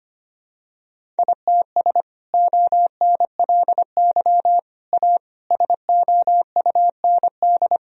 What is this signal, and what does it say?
Morse code, heard as a keyed tone, about 25 wpm, 700 Hz. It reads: ITS ONLY A SOUND
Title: Sentences from Top 1000 Words - Review Part 1 - 25wpm